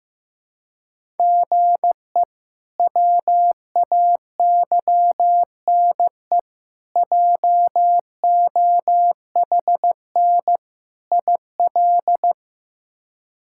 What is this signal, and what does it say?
Morse code, heard as a keyed tone, about 15 wpm, 700 Hz. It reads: GE WAYNE JOHN IL
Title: K1USN SST QSO Practice - 15wpm